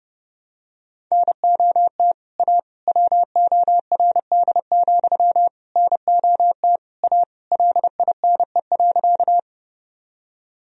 Morse code, heard as a keyed tone, about 30 wpm, 700 Hz. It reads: NOT A WORD, NOT A LINE.
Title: Sentences from Top 1000 Words - Review Part 1 - 30wpm